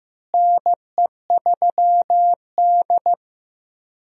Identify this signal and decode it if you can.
Morse code, heard as a keyed tone, about 15 wpm, 700 Hz. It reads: NE3D
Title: USA Call Signs - 4 Characters - Repeated 2x - 15wpm